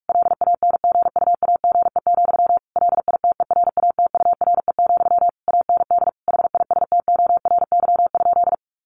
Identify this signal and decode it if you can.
Morse code, heard as a keyed tone, about 44 wpm, 700 Hz. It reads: LANGUAGE, LITERATURE, AND HISTORY?
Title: Sentences from Top 600 Words 45wpm